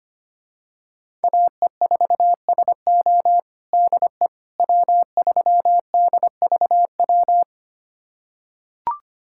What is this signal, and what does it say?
Morse code, heard as a keyed tone, about 25 wpm, 700 Hz. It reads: AE4SO DE W3DVW
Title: USA Callsign DE Callsign - 5 Characters - 25wpm